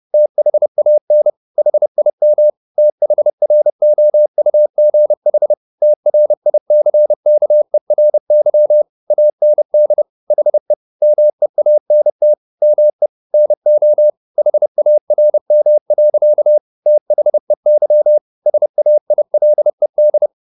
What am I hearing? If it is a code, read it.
Morse code, heard as a keyed tone, about 30 wpm, 600 Hz. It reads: THAN HIM THROUGH TRICKERY AND HE MEANT ME NO HARM. THEY SAILED